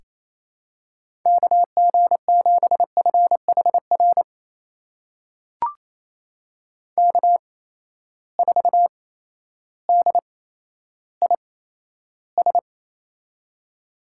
Morse code, heard as a keyed tone, about 28 wpm, spaced as for 13 wpm, 700 Hz. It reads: KG7FHR K4DIS
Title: USA Call Signs - 8x Character Spacing - 28wpm